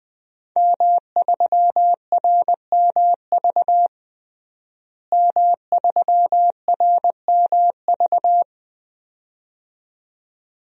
Morse code, heard as a keyed tone, about 20 wpm, 700 Hz. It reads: M3RMV M3RMV
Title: DXCC-100 Call Signs - 5 Characters - Repeated 2x - 20wpm